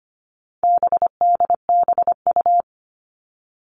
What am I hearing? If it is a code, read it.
Morse code, heard as a keyed tone, about 25 wpm, 700 Hz. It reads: BDBU
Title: All Cut Numbers - 4-Digit 25wpm